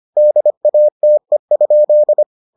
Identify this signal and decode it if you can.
Morse code, heard as a keyed tone, about 25 wpm, 600 Hz. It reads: DATE?